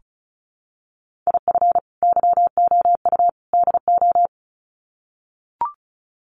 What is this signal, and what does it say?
Morse code, heard as a keyed tone, about 34 wpm, 700 Hz. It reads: IF YOU DO
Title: Sets of 3 Words 35wpm